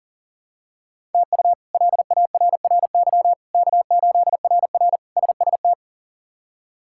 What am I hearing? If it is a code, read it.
Morse code, heard as a keyed tone, about 40 wpm, 700 Hz. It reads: TU LARRY K8RR SST
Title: K1USN SST QSO Practice - 40wpm